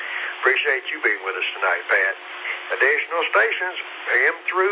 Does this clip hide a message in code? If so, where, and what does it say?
Yes: Emergency Alert System (SAME) tones from 0.8 to 4.4 s, heard as harsh buzzing bursts, with speech.